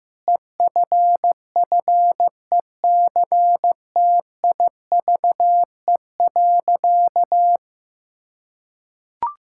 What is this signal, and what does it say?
Morse code, heard as a keyed tone, about 15 wpm, 700 Hz. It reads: EFFECTIVE.
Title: Sentences from Top 1000 Words - Review Part 6 - 15wpm